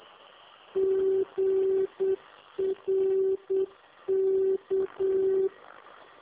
Morse code, heard as a keyed tone, about 8 wpm, 380 Hz. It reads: GRK